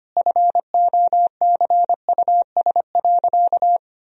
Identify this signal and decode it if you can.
Morse code, heard as a keyed tone, about 25 wpm, 700 Hz. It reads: FOCUS.
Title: Sentences from Top 1000 Words - Review Part 4 - 25wpm